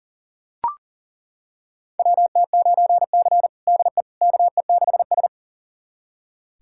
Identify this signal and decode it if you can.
Morse code, heard as a keyed tone, about 40 wpm, 700 Hz. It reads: WT9C DE KE6S
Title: USA Callsign DE Callsign - 4 Characters - 40wpm